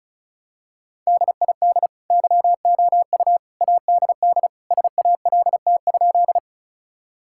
Morse code, heard as a keyed tone, about 36 wpm, 700 Hz. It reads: DID YOU ADD SALT?
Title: Sentences from Top 1000 Words - Review Part 5 - 35wpm